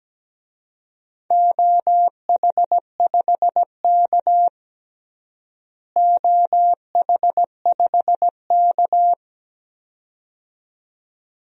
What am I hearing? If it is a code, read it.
Morse code, heard as a keyed tone, about 17 wpm, 700 Hz. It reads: OH5K OH5K
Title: DXCC-100 Call Signs - 4 Characters - Repeated 2x - 17wpm